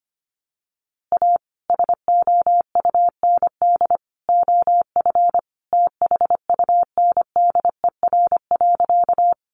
Morse code, heard as a keyed tone, about 25 wpm, 700 Hz. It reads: A SOUND OF THUNDER.
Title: A Sound Of Thunder 25wpm